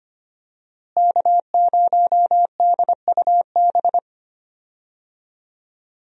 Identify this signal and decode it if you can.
Morse code, heard as a keyed tone, about 25 wpm, 700 Hz. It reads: K0DUB
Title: USA Call Signs - 5 Characters - Repeated 2x - 25wpm